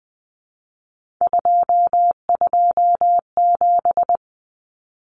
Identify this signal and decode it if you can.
Morse code, heard as a keyed tone, about 20 wpm, 700 Hz. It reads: 227
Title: Common Cut Numbers - 3-Digit 20wpm